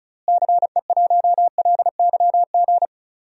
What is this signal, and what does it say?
Morse code, heard as a keyed tone, about 34 wpm, 700 Hz. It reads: CE1LYG
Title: DXCC-100 Call Signs - 6 Characters - Repeated 3x - 35wpm